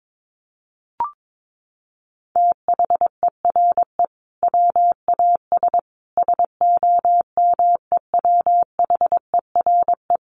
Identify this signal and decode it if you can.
Morse code, heard as a keyed tone, about 22 wpm, 700 Hz. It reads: THERE WAS SOMEWHERE
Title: Sentences from Top 1400 Words 22wpm